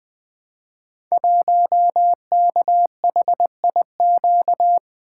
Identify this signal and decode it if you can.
Morse code, heard as a keyed tone, about 20 wpm, 700 Hz. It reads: 1KHIQ
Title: Code Groups - 5 Letter:Numbers - 20wpm